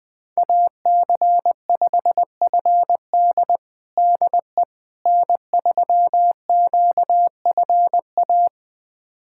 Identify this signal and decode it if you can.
Morse code, heard as a keyed tone, about 20 wpm, 700 Hz. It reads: AC5FD DE N3QFA
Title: USA Callsign DE Callsign - 5 Characters - 20wpm